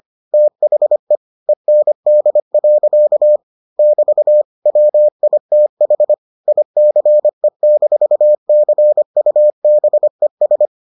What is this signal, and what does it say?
Morse code, heard as a keyed tone, about 25 wpm, 600 Hz. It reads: THE END. <BT> WITH ICE-CUBES